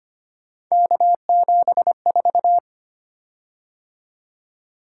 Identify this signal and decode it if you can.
Morse code, heard as a keyed tone, about 25 wpm, 700 Hz. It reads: K74
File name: Three Letter-Number Combinations - 25wpm